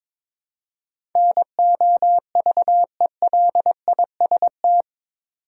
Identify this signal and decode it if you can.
Morse code, heard as a keyed tone, about 22 wpm, 700 Hz. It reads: NOVELIST
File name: practice / 22wpm / Eight Letter Words - 22wpm